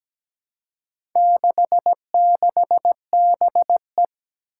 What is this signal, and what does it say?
Morse code, heard as a keyed tone, about 17 wpm, 700 Hz. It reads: 66BE
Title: Four Challenging Letter-Number Combinations 17wpm